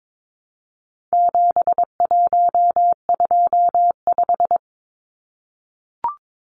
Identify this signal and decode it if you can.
Morse code, heard as a keyed tone, about 22 wpm, 700 Hz. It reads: 7125
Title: Numbers - 4-Digit 22wpm